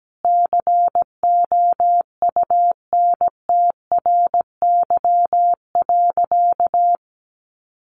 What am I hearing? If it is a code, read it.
Morse code, heard as a keyed tone, about 17 wpm, 700 Hz. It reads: COUNTRY.
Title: Sentences from Top 300 Words 17wpm